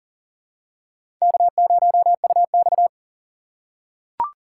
Morse code, heard as a keyed tone, about 39 wpm, 700 Hz. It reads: K0UX
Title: USA Call Signs - 4 Characters - 2x Character Spacing - 40wpm